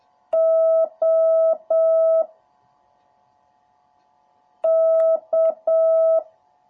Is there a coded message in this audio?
Yes: Morse code from 0.3 to 6.2 s, heard as a keyed tone, about 7 wpm, 650 Hz. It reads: O K